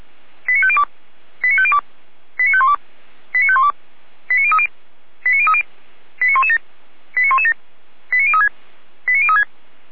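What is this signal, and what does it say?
Five-tone selective call (ZVEI), heard as quick runs of five tones, tones of about 70 ms. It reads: 89502 89502 89512 89512 89030 89030 89117 89117 89036 89036
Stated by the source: Received on an SDR, ZVEI five-tone selective call.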